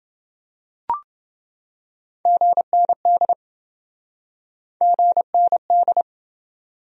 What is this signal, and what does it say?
Morse code, heard as a keyed tone, about 27 wpm, 700 Hz. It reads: GND GND